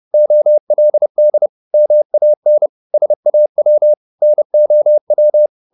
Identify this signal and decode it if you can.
Morse code, heard as a keyed tone, about 30 wpm, 600 Hz. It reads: OLD MAN SAW NOW
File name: the old man and the sea / the old man and the sea 1 30wpm